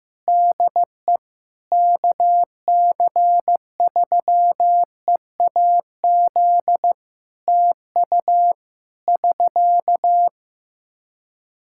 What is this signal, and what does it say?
Morse code, heard as a keyed tone, about 15 wpm, 700 Hz. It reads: DE KC3EAZ TU <SK>
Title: Practice QSO - Exchange 4 - Ending - 15wpm